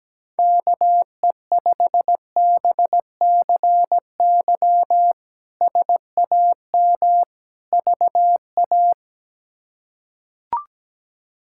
Morse code, heard as a keyed tone, about 17 wpm, 700 Hz. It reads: KE5BCY SAM VA
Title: K1USN SST QSO Practice - 17wpm